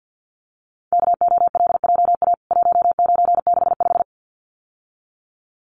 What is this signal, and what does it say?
Morse code, heard as a keyed tone, about 50 wpm, 700 Hz. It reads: KOLJA 1965
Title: CWT Exchange 50wpm